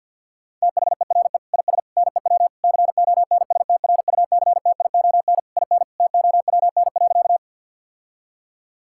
Morse code, heard as a keyed tone, about 50 wpm, 700 Hz. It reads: THERE IS NEW CONSTRUCTION IN TOWN.
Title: Sentences from Top 1600 Words 50wpm